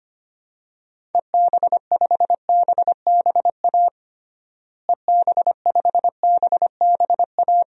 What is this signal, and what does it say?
Morse code, heard as a keyed tone, about 25 wpm, 700 Hz. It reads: EB5BBA EB5BBA
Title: DXCC-100 Call Signs - 6 Characters - Repeated 3x - 25wpm